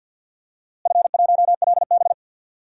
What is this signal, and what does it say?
Morse code, heard as a keyed tone, about 52 wpm, 700 Hz. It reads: F1LB
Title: DXCC-100 Call Signs - 4 Characters - Repeated 2x - 50wpm